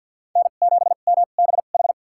Morse code, heard as a keyed tone, about 52 wpm, spaced as for 37 wpm, 700 Hz. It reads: N7MBH
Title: USA Call Signs - 5 Characters - 2x Character Spacing - 50wpm